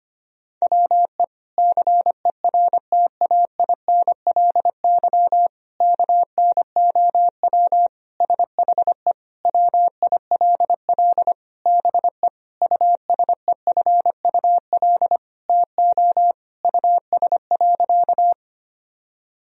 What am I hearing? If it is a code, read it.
Morse code, heard as a keyed tone, about 25 wpm, 700 Hz. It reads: WE CERTAINLY KNOW SHE WILL BE USEFUL TO US.
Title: Sentences from Top 1000 Words - Review Part 4 - 25wpm